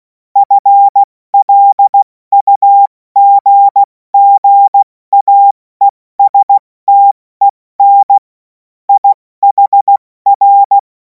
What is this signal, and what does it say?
Morse code, heard as a keyed tone, about 16 wpm, 800 Hz. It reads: FLUGGAESTEN IHR
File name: Morsekurs / Witze / Witz 31. März 2026 70 F80